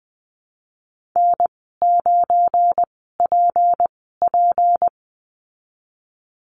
Morse code, heard as a keyed tone, about 20 wpm, spaced as for 15 wpm, 700 Hz. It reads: N9PP
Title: 186 Lesson 30g - Letter b - Call Signs 20wpm